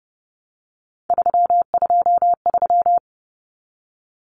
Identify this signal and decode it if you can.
Morse code, heard as a keyed tone, about 30 wpm, 700 Hz. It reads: 323
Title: Common Cut Numbers - 3-Digit 30wpm